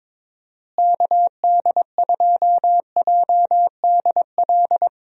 Morse code, heard as a keyed tone, about 22 wpm, 700 Hz. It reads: KD2JDL
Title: USA Call Signs - 6 Characters - Repeated 3x - 22wpm